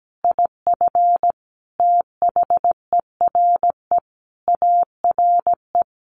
Morse code, heard as a keyed tone, about 17 wpm, 700 Hz. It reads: IF THERE ARE
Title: Sentences from Top 1500 Words 17wpm